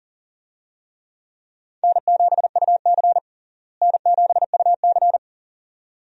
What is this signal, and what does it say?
Morse code, heard as a keyed tone, about 40 wpm, 700 Hz. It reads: N7UC N7UC
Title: USA Call Signs - 4 Characters - Repeated 2x - 40wpm